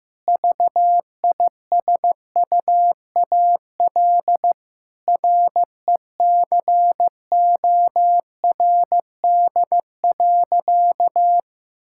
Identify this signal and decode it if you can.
Morse code, heard as a keyed tone, about 15 wpm, 700 Hz. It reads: VISUAL RECORD.
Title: Sentences from Top 1000 Words - Review Part 6 - 15wpm